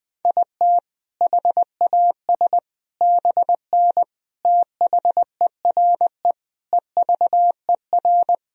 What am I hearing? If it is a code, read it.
Morse code, heard as a keyed tone, about 20 wpm, 700 Hz. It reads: IT HAS BN THERE EVER
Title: Sentences from Top 400 Words and Abbreviations 20wpm